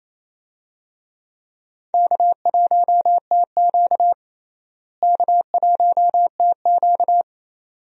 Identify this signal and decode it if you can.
Morse code, heard as a keyed tone, about 28 wpm, 700 Hz. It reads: K1TQ K1TQ
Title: USA Call Signs - 4 Characters - Repeated 2x - 28wpm